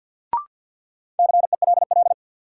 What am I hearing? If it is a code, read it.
Morse code, heard as a keyed tone, about 53 wpm, 700 Hz. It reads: CELL
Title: Sentences from Top 1100 Words - New Word Drill Alt - 50wpm